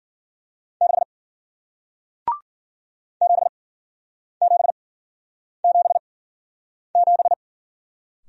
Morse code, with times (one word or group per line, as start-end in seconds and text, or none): 0.8-1.0 6
3.2-7.3 7777